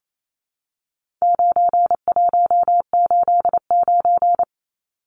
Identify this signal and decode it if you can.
Morse code, heard as a keyed tone, about 28 wpm, 700 Hz. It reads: 9189
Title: Numbers - 4-Digit 28wpm